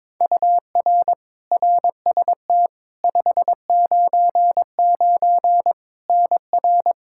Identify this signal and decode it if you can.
Morse code, heard as a keyed tone, about 22 wpm, 700 Hz. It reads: UR RST 599 NR
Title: Practice QSO - Exchange 1 - Beginning - 22wpm